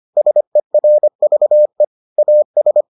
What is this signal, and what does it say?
Morse code, heard as a keyed tone, about 25 wpm, 600 Hz. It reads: SERVE AS